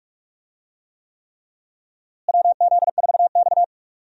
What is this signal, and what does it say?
Morse code, heard as a keyed tone, about 42 wpm, 700 Hz. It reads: WZ4X
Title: USA Call Signs - 4 Characters - Repeated 3x - 45wpm